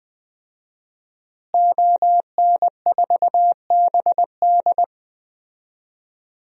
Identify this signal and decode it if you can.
Morse code, heard as a keyed tone, about 20 wpm, 700 Hz. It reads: ON4BD
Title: DXCC-100 Call Signs - 5 Characters - Repeated 2x - 20wpm